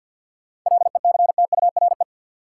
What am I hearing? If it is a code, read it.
Morse code, heard as a keyed tone, about 53 wpm, 700 Hz. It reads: LECTURE
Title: Sentences from Top 1600 Words - New Word Drill Alt - 50wpm